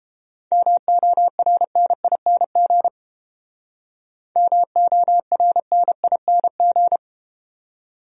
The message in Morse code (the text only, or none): MORNING MORNING